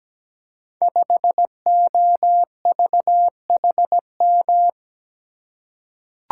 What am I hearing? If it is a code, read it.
Morse code, heard as a keyed tone, about 17 wpm, 700 Hz. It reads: 5OVHM
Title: Code Groups - 5 Letter:Numbers - 17wpm